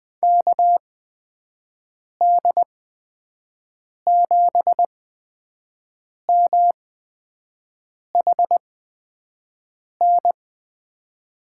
Morse code, heard as a keyed tone, about 20 wpm, spaced as for 5 wpm, 700 Hz. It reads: KD7MHN
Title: USA Call Signs - 6 Characters - 8x Character Spacing - 20wpm